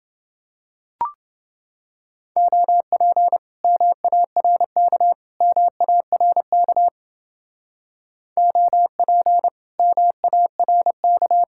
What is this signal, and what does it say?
Morse code, heard as a keyed tone, about 29 wpm, 700 Hz. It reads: OP MARK MARK OP MARK